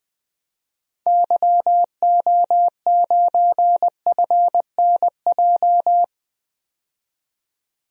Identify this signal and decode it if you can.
Morse code, heard as a keyed tone, about 20 wpm, 700 Hz. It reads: YO9FNJ
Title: DXCC-100 Call Signs - 6 Characters - 20wpm